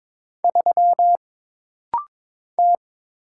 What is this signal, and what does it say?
Morse code, heard as a keyed tone, about 22 wpm, 700 Hz. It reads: 3 T
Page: This rapid-fire timing is geared to push your ICR (Instant Character Recognition) skills.